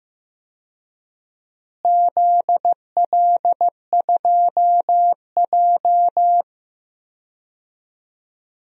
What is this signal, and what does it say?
Morse code, heard as a keyed tone, about 15 wpm, 700 Hz. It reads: ZL2J
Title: DXCC-100 Call Signs - 4 Characters - Repeated 3x - 15wpm